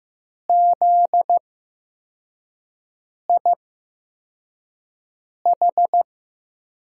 Morse code, heard as a keyed tone, about 15 wpm, spaced as for 4 wpm, 700 Hz. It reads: ZIH
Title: USA Call Signs - 6 Characters - 8x Character Spacing - 15wpm